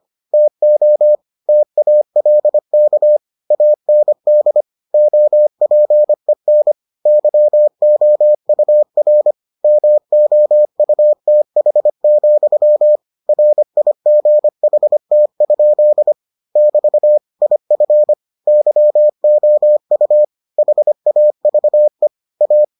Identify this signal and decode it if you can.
Morse code, heard as a keyed tone, about 25 wpm, 600 Hz. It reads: TO TALK AND OPEN YOUR MOUTH, RIGHT? <BT> IF YOU HAVE A